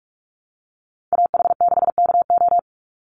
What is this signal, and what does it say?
Morse code, heard as a keyed tone, about 44 wpm, 700 Hz. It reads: AH6KO
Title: Call Signs Repeated 3x - 45wpm